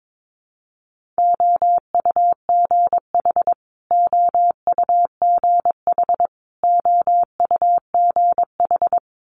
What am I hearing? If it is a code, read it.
Morse code, heard as a keyed tone, about 22 wpm, 700 Hz. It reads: OUGH OUGH OUGH
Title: Multi-letter Phonemes Repeated 3x 22wpm